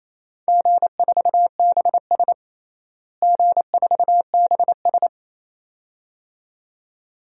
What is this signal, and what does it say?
Morse code, heard as a keyed tone, about 28 wpm, 700 Hz. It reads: G4BS G4BS
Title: DXCC-100 Call Signs - 4 Characters - Repeated 2x - 28wpm